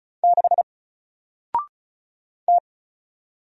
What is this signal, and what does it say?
Morse code, heard as a keyed tone, about 36 wpm, 700 Hz. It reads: 6 T
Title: Single Character (Letter:Number:Punctuation:Prosign) Alt 35wpm